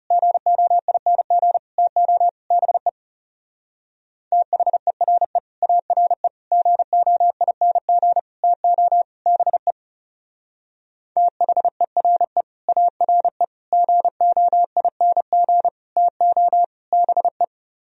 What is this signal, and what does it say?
Morse code, heard as a keyed tone, about 37 wpm, 700 Hz. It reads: GOING TO BE THERE ARE GOING TO BE THERE ARE GOING TO BE
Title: Speed-Racing - Sets of 5 Words 30wpm